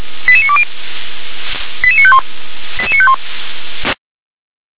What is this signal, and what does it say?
Five-tone selective call (ZVEI), heard as quick runs of five tones, tones of about 70 ms. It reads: 88920 88951 88951